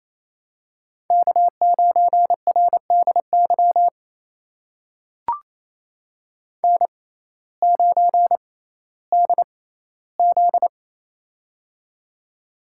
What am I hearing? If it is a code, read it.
Morse code, heard as a keyed tone, about 27 wpm, spaced as for 15 wpm, 700 Hz. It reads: K9RDY N9DZ
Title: USA Call Signs - 6x Character Spacing - 28wpm